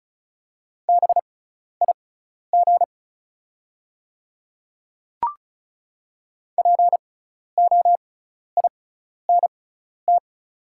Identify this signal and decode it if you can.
Morse code, heard as a keyed tone, about 36 wpm, spaced as for 12 wpm, 700 Hz. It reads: BIG POINT
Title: 6x Character Spacing - Top 300 Words - 35wpm